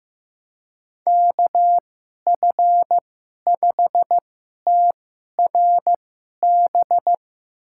Morse code, heard as a keyed tone, about 15 wpm, spaced as for 11 wpm, 700 Hz. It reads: KF5TRB
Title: USA Call Signs - 2x Character Spacing - 15wpm